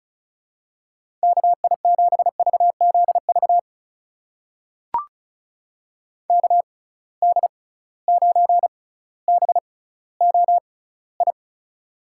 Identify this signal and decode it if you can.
Morse code, heard as a keyed tone, about 36 wpm, spaced as for 18 wpm, 700 Hz. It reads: KI7VZV KD9BOI
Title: USA Call Signs - 6 Characters - 6x Character Spacing - 35wpm